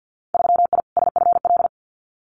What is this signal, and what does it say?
Morse code, heard as a keyed tone, about 50 wpm, 700 Hz. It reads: 3I SFL
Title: Winter Field Day Exchanges 50wpm